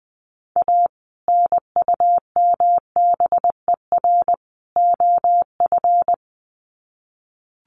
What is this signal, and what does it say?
Morse code, heard as a keyed tone, about 20 wpm, 700 Hz. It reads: A NUMBER OF
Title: Sets of 3 Words 20wpm